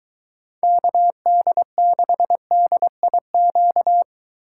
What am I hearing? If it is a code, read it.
Morse code, heard as a keyed tone, about 24 wpm, 700 Hz. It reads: KD6DIQ